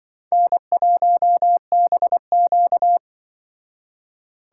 Morse code, heard as a keyed tone, about 23 wpm, 700 Hz. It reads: N1BQ
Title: Speed Racing - USA 1x2 Call Signs 20wpm